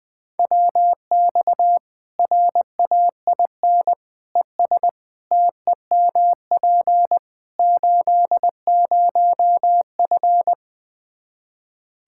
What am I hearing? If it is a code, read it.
Morse code, heard as a keyed tone, about 20 wpm, 700 Hz. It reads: WX RAIN ES TEMP 80F
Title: K1USN SST QSO Practice 20wpm